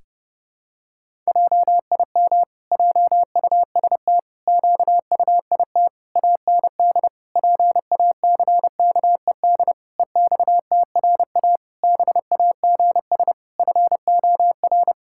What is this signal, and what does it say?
Morse code, heard as a keyed tone, about 30 wpm, 700 Hz. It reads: JIM JUST QUIT AND PACKED EXTRA BAGS FOR
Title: Pangram Challenge v2 - 30wpm